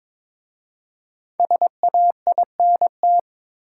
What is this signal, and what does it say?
Morse code, heard as a keyed tone, about 22 wpm, 700 Hz. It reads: SAINT
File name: 5 Letter Words - First 3 Letters Given - 22wpm